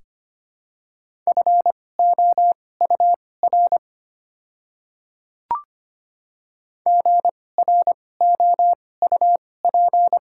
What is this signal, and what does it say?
Morse code, heard as a keyed tone, about 25 wpm, spaced as for 18 wpm, 700 Hz. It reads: FOUR GROUP
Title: 2x Character Spacing - Top 300 Words - 25wpm